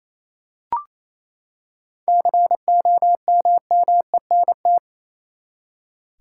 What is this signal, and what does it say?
Morse code, heard as a keyed tone, about 28 wpm, 700 Hz. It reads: COMMENT
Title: Seven Letter Words - 28wpm